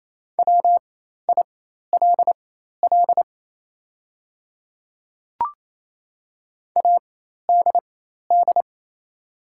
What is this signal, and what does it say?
Morse code, heard as a keyed tone, about 28 wpm, spaced as for 13 wpm, 700 Hz. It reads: WILL ADD
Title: 4x Character Spacing - Top 200 Words - 28wpm